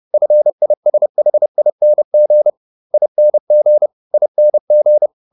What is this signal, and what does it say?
Morse code, heard as a keyed tone, about 30 wpm, 600 Hz. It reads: FISHING ING ING